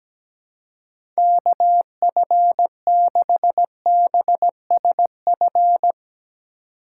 Morse code, heard as a keyed tone, about 17 wpm, 700 Hz. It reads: KF6BSF
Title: USA Call Signs - 6 Characters - 2x Character Spacing - 17wpm